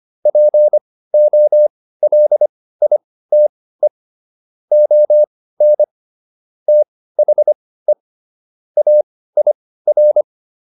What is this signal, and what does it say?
Morse code, heard as a keyed tone, about 25 wpm, spaced as for 16 wpm, 600 Hz. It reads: POLITE ON THE AIR